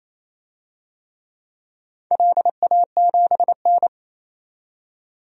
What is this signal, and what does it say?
Morse code, heard as a keyed tone, about 29 wpm, 700 Hz. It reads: LA7N
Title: DXCC-100 Call Signs - 4 Characters - 28wpm